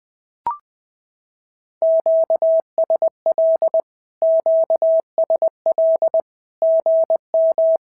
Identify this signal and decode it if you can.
Morse code, heard as a keyed tone, about 21 wpm, 650 Hz. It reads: QSL QSL GM
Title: SOTA QSO Practice 20wpm